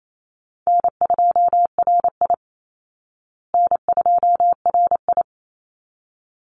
Morse code, heard as a keyed tone, about 28 wpm, 700 Hz. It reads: N2RI N2RI
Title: Easy Call Signs Repeated 2x - 28wpm